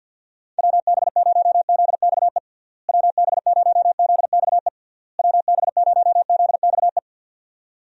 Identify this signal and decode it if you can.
Morse code, heard as a keyed tone, about 51 wpm, 700 Hz. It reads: WB0ZXE WB0ZXE WB0ZXE
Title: USA Call Signs - 6 Characters - Repeated 3x - 50wpm